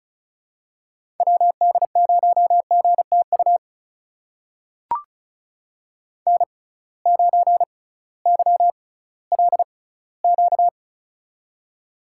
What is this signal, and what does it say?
Morse code, heard as a keyed tone, about 36 wpm, spaced as for 19 wpm, 700 Hz. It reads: WD0GTU N9YLQ
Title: USA Call Signs - 6x Character Spacing - 35wpm